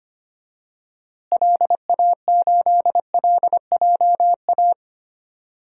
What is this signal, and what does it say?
Morse code, heard as a keyed tone, about 25 wpm, 700 Hz. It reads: LA8LJA
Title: DXCC-100 Call Signs - 6 Characters - Repeated 2x - 25wpm